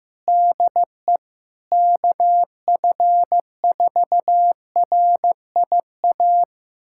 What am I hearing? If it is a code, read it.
Morse code, heard as a keyed tone, about 16 wpm, 700 Hz. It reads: DE KF4RIA
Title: USA Callsign DE Callsign 15wpm